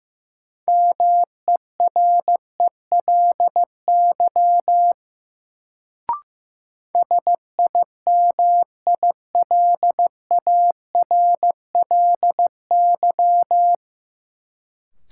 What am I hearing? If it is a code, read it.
Morse code, heard as a keyed tone, about 15 wpm, 700 Hz. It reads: MERELY SIMILARLY
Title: Sentences from Top 1400 Words - New Word Drill - 15wpm